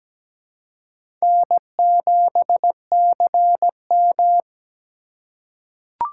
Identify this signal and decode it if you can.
Morse code, heard as a keyed tone, about 17 wpm, 700 Hz. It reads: N7CM
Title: USA Call Signs - 4 Characters - 4x Character Spacing - 17wpm